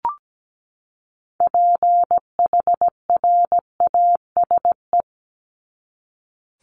Morse code, heard as a keyed tone, about 17 wpm, 700 Hz. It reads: PHRASE
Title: Top 1501-1750 Words 17wpm